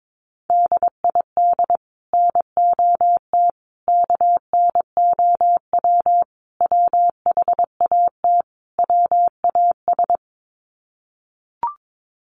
Morse code, heard as a keyed tone, about 22 wpm, 700 Hz. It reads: DID NOT KNOW WHAT WAS
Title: Sets of 5 Words Encore2 22wpm